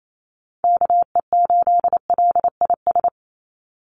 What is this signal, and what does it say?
Morse code, heard as a keyed tone, about 28 wpm, 700 Hz. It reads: KE8LIS
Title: USA Call Signs 28wpm